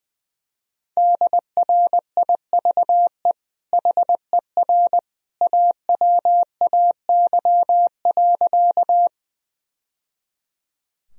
Morse code, heard as a keyed tone, about 19 wpm, 700 Hz. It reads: DRIVE HER AWAY.
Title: Sentences from Top 1000 Words - Review Part 2 - 20wpm